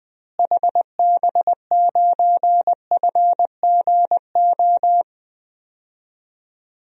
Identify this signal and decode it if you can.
Morse code, heard as a keyed tone, about 20 wpm, 700 Hz. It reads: HB9FGO
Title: DXCC-100 Call Signs - 6 Characters - Repeated 2x - 20wpm